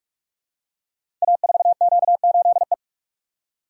Morse code, heard as a keyed tone, about 44 wpm, 700 Hz. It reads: A4Q8E
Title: Code Groups - 5 Letter:Numbers - 45wpm